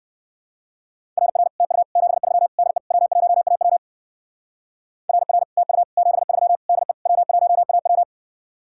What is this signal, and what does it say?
Morse code, heard as a keyed tone, about 56 wpm, 700 Hz. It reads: RR TU 73 DE W1AW RR TU 73 DE W1AW
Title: Speed-Racing Common QSO Phrases - 45wpm